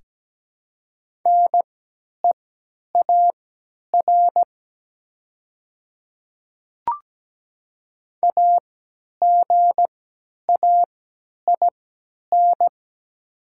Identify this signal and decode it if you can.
Morse code, heard as a keyed tone, about 17 wpm, spaced as for 10 wpm, 700 Hz. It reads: NEAR AGAIN
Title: 3x Character Spacing - Top 300 Words - 17wpm